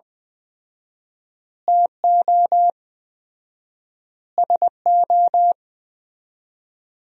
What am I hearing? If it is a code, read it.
Morse code, heard as a keyed tone, about 21 wpm, 700 Hz. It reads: TO SO
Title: 4x Word-Spacing - Sentences from Top 400 Words - 20wpm